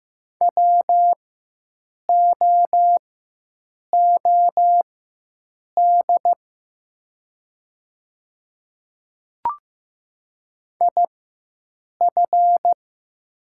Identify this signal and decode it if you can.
Morse code, heard as a keyed tone, about 15 wpm, spaced as for 7 wpm, 700 Hz. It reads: WOOD IF
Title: 4x Character Spacing - Top 300 Words - 15wpm